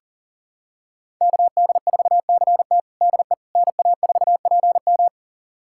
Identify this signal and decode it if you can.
Morse code, heard as a keyed tone, about 40 wpm, 700 Hz. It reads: KD4CT DE NA4PM
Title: USA Callsign DE Callsign - 5 Characters - 40wpm